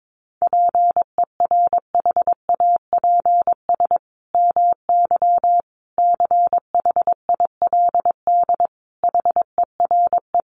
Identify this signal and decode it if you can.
Morse code, heard as a keyed tone, about 22 wpm, 700 Hz. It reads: PERHAPS MY CHILD HERE
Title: Sentences from Top 1100 Words 22wpm